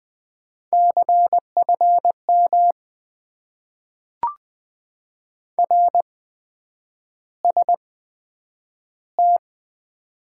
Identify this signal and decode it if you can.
Morse code, heard as a keyed tone, about 20 wpm, spaced as for 9 wpm, 700 Hz. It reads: CFM RST
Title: Top 90 QSO Elements - 8x Character Spacing - 20wpm